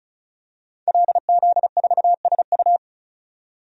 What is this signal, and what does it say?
Morse code, heard as a keyed tone, about 34 wpm, 700 Hz. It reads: LZ4SU
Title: DXCC-100 Call Signs - 5 Characters - 35wpm